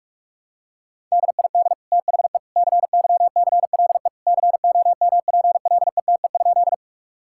Morse code, heard as a keyed tone, about 41 wpm, 700 Hz. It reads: DID THE CYCLE COMPLETE?
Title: Sentences from Top 1000 Words - Review Part 6 - 45wpm